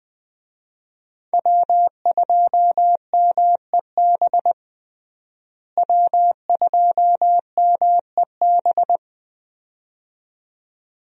Morse code, heard as a keyed tone, about 20 wpm, 700 Hz. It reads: W2MEB W2MEB
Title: USA Call Signs - 5 Characters - Repeated 2x - 20wpm